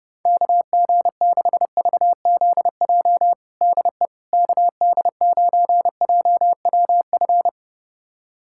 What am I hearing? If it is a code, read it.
Morse code, heard as a keyed tone, about 32 wpm, 700 Hz. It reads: KG6VZJ DE KD9JWF
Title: USA Callsign DE Callsign - 6 Characters - 30wpm